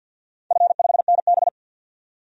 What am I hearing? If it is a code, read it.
Morse code, heard as a keyed tone, about 52 wpm, 700 Hz. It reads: F5NB
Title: DXCC-100 Call Signs - 4 Characters - 50wpm